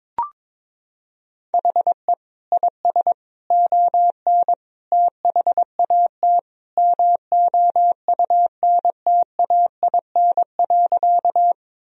Morse code, heard as a keyed tone, about 22 wpm, 700 Hz. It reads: HE IS ON THAT MOUNTAIN.
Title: Sentences from Top 1000 Words - Review Part 1 - 22wpm